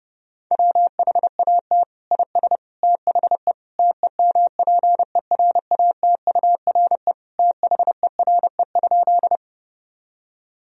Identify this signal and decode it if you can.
Morse code, heard as a keyed tone, about 30 wpm, 700 Hz. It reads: WHAT IS THE TEMPERATURE THERE?
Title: Sentences from Top 1000 Words - Review Part 4 - 30wpm